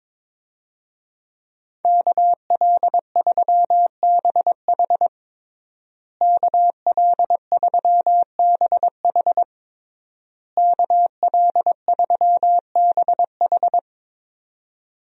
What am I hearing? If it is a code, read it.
Morse code, heard as a keyed tone, about 22 wpm, 700 Hz. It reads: KL3BH KL3BH KL3BH
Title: USA Call Signs - 5 Characters - Repeated 3x - 22wpm